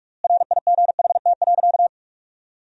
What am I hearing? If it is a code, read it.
Morse code, heard as a keyed tone, about 46 wpm, 700 Hz. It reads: RIGHT.
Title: 4x Word-Spacing - Sentences from Top 400 Words - 45wpm